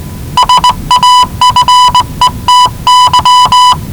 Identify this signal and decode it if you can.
Morse code, heard as a keyed tone, about 19 wpm, 990 Hz. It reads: SAFETY
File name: WellingtonRadioZlwCloseDownOnHfAndMf / 01-Zlw 4277khz 417.5 qrt until 0925z